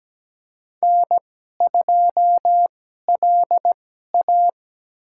Morse code, heard as a keyed tone, about 17 wpm, spaced as for 12 wpm, 700 Hz. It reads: N2LA
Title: USA Call Signs - 4 Characters - 2x Character Spacing - 17wpm